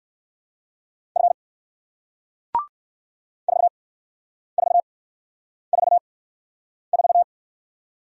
Morse code, heard as a keyed tone, about 58 wpm, spaced as for 9 wpm, 700 Hz. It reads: F 4444